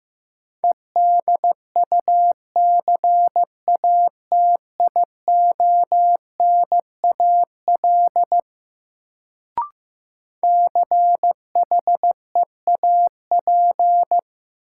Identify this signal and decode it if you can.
Morse code, heard as a keyed tone, about 15 wpm, 700 Hz. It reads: EDUCATIONAL CHEAP
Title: Sentences from Top 1200 Words - New Word Drill Alt - 15wpm